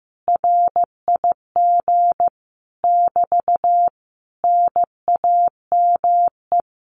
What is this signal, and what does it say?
Morse code, heard as a keyed tone, about 15 wpm, 700 Hz. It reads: RIG <BT> NAME
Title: Common QSO Phrases Encore 15wpm